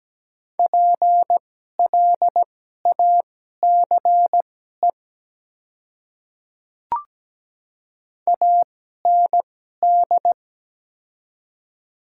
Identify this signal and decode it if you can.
Morse code, heard as a keyed tone, about 17 wpm, spaced as for 12 wpm, 700 Hz. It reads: PLACE AND
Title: 2x Character Spacing - Top 300 Words - 17wpm